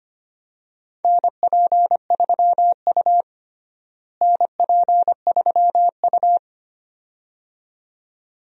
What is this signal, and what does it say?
Morse code, heard as a keyed tone, about 25 wpm, 700 Hz. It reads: NP3U NP3U
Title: USA Call Signs - 4 Characters - Repeated 2x - 25wpm